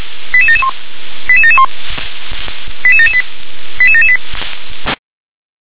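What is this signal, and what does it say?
Five-tone selective call (ZVEI), heard as quick runs of five tones, tones of about 70 ms. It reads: 88771 88771 88778 88778
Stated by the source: ZVEI five-tone call 88951 received by SDR.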